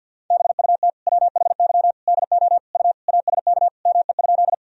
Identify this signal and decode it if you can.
Morse code, heard as a keyed tone, about 52 wpm, 700 Hz. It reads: BUT WHY DO U ASK ME?
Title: Sentences from Top 200 Words and Abbreviations 50wpm